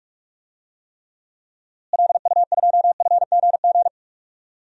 Morse code, heard as a keyed tone, about 46 wpm, 700 Hz. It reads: LU2FGG
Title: DXCC-100 Call Signs - 6 Characters - Repeated 2x - 45wpm